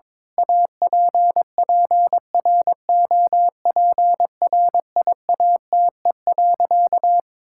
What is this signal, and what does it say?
Morse code, heard as a keyed tone, about 21 wpm, 700 Hz. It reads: APPROPRIATE.
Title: Sentences from Top 1000 Words - Review Part 5 - 22wpm